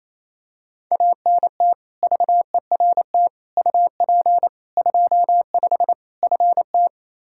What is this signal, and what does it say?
Morse code, heard as a keyed tone, about 28 wpm, 700 Hz. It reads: ANT VERT UP 25 FT
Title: Practice QSO - Exchange 2 - Info - 28wpm